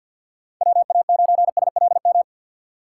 Morse code, heard as a keyed tone, about 51 wpm, 700 Hz. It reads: WA9SLM
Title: USA Call Signs - 3x Character Spacing - 50wpm